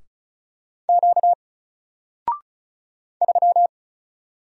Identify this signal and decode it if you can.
Morse code, heard as a keyed tone, about 35 wpm, 700 Hz. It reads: Q 3
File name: Single-Letter-Number-35wpm